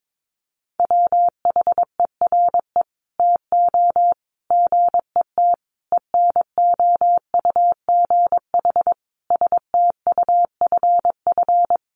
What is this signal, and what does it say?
Morse code, heard as a keyed tone, about 22 wpm, 700 Hz. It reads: WHERE TO GET ENOUGH STUFF
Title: Sentences from Top 1400 Words 22wpm